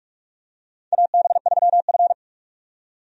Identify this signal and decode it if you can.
Morse code, heard as a keyed tone, about 41 wpm, 700 Hz. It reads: AB3F
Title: USA Call Signs - 4 Characters - 45wpm